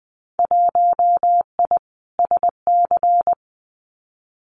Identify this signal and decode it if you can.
Morse code, heard as a keyed tone, about 19 wpm, 700 Hz. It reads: 1I SC
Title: Winter Field Day Exchanges Encore 20wpm